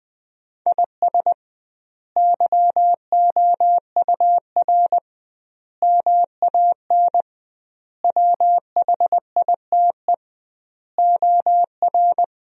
Morse code, heard as a keyed tone, about 20 wpm, 700 Hz. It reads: IS YOUR MAN WHITE OR
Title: 2x Word-Spacing - Sentences from Top 400 Words - 20wpm